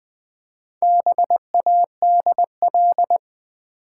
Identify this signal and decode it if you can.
Morse code, heard as a keyed tone, about 20 wpm, 700 Hz. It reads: BADL
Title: Build A Word - 1 Letter At A Time - 5-Letter Words - 20wpm